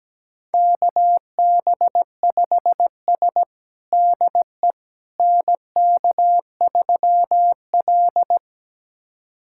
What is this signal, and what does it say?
Morse code, heard as a keyed tone, about 17 wpm, 700 Hz. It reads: KB5S DE NK3L
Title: USA Callsign DE Callsign - 4 Characters - 17wpm